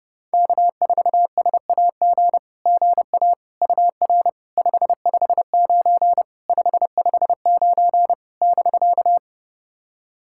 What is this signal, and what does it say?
Morse code, heard as a keyed tone, about 28 wpm, 690 Hz. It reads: K4SAG GA UR 559 559 <BK>
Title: SOTA QSO Practice 30wpm